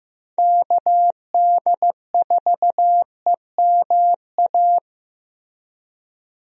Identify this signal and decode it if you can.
Morse code, heard as a keyed tone, about 15 wpm, 700 Hz. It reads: KD4EMA
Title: USA Call Signs - 6 Characters - Repeated 2x - 15wpm